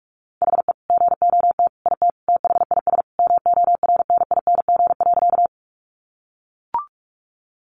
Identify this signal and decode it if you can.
Morse code, heard as a keyed tone, about 44 wpm, 700 Hz. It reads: HE GOT IT THIS MORNING.
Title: Sentences from Top 400 Words 45wpm